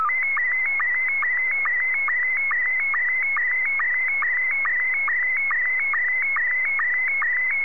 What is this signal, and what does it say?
SSTV (slow-scan television), Scottie 1.